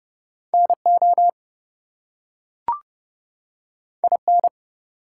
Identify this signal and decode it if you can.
Morse code, heard as a keyed tone, about 30 wpm, 700 Hz. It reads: NO IN
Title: Two Letter Words - 30wpm